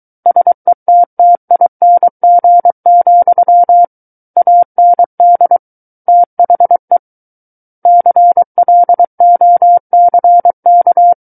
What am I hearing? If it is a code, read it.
Morse code, heard as a keyed tone, about 23 wpm, 700 Hz. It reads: SETTING, AND THE CLOCK
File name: Book One / 14-Chapter 14 - 23 WPM